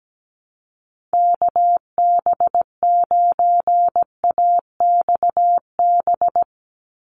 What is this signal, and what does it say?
Morse code, heard as a keyed tone, about 17 wpm, 700 Hz. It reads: KB9AXB
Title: USA Call Signs 17wpm